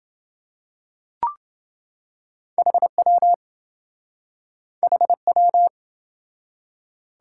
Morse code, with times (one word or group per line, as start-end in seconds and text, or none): 2.6-3.3 HW
4.8-5.7 HW